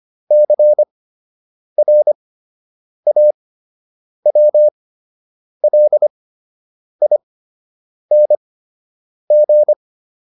Morse code, heard as a keyed tone, about 25 wpm, spaced as for 8 wpm, 600 Hz. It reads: CRAWLING